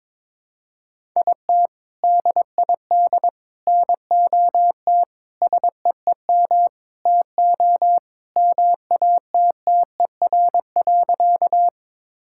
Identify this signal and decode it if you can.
Morse code, heard as a keyed tone, about 22 wpm, 700 Hz. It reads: IT DID NOT SEEM TO MATTER.
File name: practice / 22wpm / Sentences with 4-Word Phrases - 22wpm